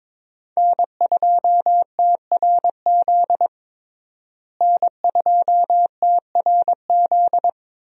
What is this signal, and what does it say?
Morse code, heard as a keyed tone, about 21 wpm, 700 Hz. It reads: N2TRZ N2TRZ
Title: USA Call Signs - 5 Characters - Repeated 2x - 22wpm